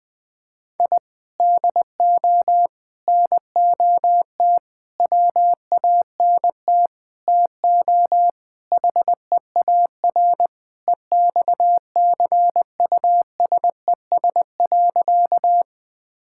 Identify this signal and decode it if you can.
Morse code, heard as a keyed tone, about 21 wpm, 700 Hz. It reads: I DO NOT WANT TO HEAR EXCUSES.
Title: Sentences with 5-Word Phrases - 20wpm